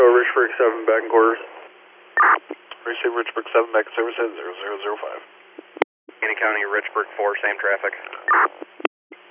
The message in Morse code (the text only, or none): none